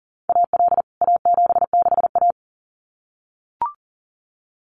Morse code, heard as a keyed tone, about 40 wpm, 700 Hz. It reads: AL A76A
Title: CWT Exchange 40wpm